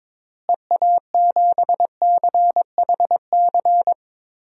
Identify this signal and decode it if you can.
Morse code, heard as a keyed tone, about 22 wpm, 700 Hz. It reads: EA7CHC
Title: DXCC-100 Call Signs - 6 Characters - Repeated 3x - 22wpm